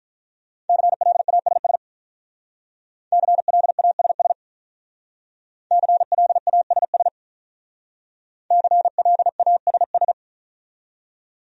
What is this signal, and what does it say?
Morse code, heard as a keyed tone, about 45 wpm, 700 Hz. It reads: CLASS CLASS CLASS CLASS